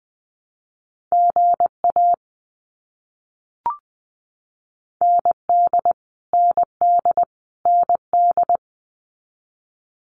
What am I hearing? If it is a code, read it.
Morse code, heard as a keyed tone, about 20 wpm, 700 Hz. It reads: GA ND ND ND
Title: US States Repeated 3x 20wpm